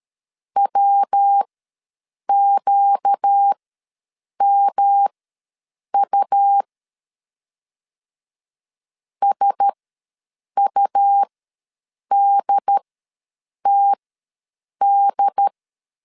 Morse code, heard as a keyed tone, about 13 wpm, spaced as for 7 wpm, 800 Hz. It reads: WQMU SUDTD
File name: LEZIONI / 5 LBVC / 5 L ws 12wpm 2958